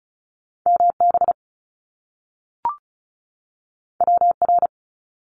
Morse code, heard as a keyed tone, about 36 wpm, 700 Hz. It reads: MB WR
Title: Top Two Letter Combinations in Words 35wpm